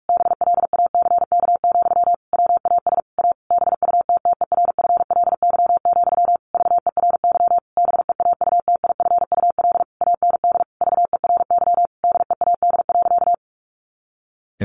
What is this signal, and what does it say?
Morse code, heard as a keyed tone, about 44 wpm, 700 Hz. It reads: BLACK, WAS A BUTTERFLY, VERY BEAUTIFUL AND VERY DEAD.
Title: A Sound Of Thunder 45wpm